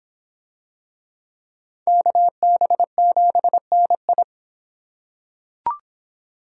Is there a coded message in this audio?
Yes: Morse code from 1.9 to 4.2 s, heard as a keyed tone, about 24 wpm, 700 Hz. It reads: KB7NI